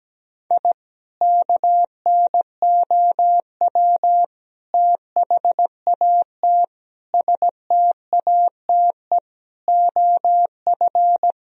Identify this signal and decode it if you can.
Morse code, heard as a keyed tone, about 17 wpm, 700 Hz. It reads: I KNOW THAT STATE OF
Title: Sentences from Top 1000 Words - Review Part 2 - 17wpm